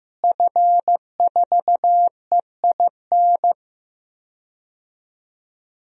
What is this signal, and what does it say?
Morse code, heard as a keyed tone, about 15 wpm, 700 Hz. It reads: F4EIN
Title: DXCC-100 Call Signs - 5 Characters - Repeated 2x - 15wpm